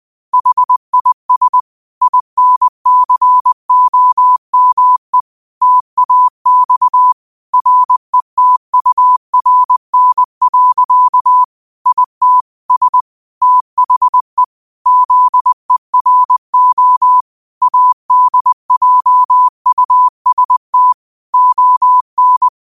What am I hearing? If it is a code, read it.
Morse code, heard as a keyed tone, about 20 wpm, 1000 Hz. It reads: HIS INCOME TAX RETURN. IT S THE ZERO ADJUST ON
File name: qotdfiles / 20wpm